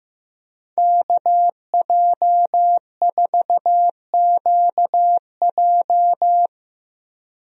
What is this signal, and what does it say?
Morse code, heard as a keyed tone, about 15 wpm, 700 Hz. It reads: KJ4QJ
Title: USA Call Signs - 5 Characters - 15wpm